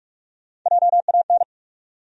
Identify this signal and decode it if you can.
Morse code, heard as a keyed tone, about 43 wpm, 700 Hz. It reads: JAN
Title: USA Common Names - Max 5 Letters - 45wpm